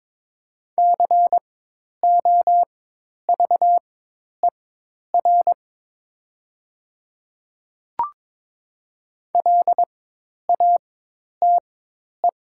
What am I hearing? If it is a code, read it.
Morse code, heard as a keyed tone, about 22 wpm, spaced as for 10 wpm, 700 Hz. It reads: COVER LATE
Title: 4x Character Spacing - Top 300 Words - 22wpm